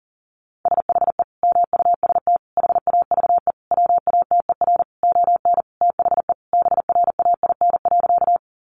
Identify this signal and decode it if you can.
Morse code, heard as a keyed tone, about 40 wpm, 700 Hz. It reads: SHE MUST HAVE WATER ON THE BRAIN.
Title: Sentences from Top 1100 Words 40wpm